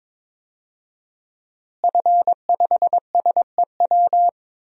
Morse code, heard as a keyed tone, about 22 wpm, 700 Hz. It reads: F5SEW
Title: DXCC-100 Call Signs - 5 Characters - Repeated 2x - 22wpm